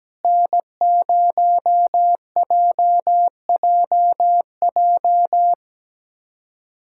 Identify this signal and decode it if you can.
Morse code, heard as a keyed tone, about 17 wpm, 700 Hz. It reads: N0JJJ
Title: USA Callsign DE Callsign - 5 Characters - 17wpm